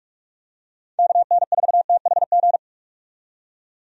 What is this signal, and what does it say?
Morse code, heard as a keyed tone, about 43 wpm, 700 Hz. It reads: KN4THG
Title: USA Call Signs - 6 Characters - 45wpm